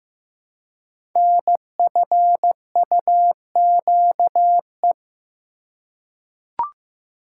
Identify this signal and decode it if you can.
Morse code, heard as a keyed tone, about 15 wpm, 700 Hz. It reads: NFUQE
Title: Code Groups - 5 Letter:Numbers - 15wpm